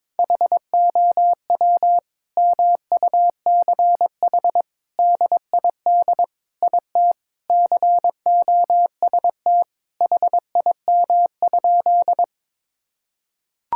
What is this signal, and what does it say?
Morse code, heard as a keyed tone, about 22 wpm, 700 Hz. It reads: HOW MUCH DID IT COST HIM?
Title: Sentences from Top 1000 Words - Review Part 5 - 22wpm